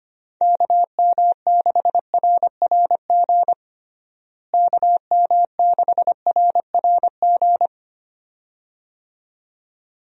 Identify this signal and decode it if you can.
Morse code, heard as a keyed tone, about 25 wpm, 700 Hz. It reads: KM6RRG KM6RRG
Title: USA Call Signs - 6 Characters - Repeated 2x - 25wpm